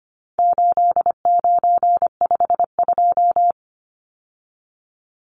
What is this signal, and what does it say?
Morse code, heard as a keyed tone, about 25 wpm, 700 Hz. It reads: 8952
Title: Numbers - 4-Digit 25wpm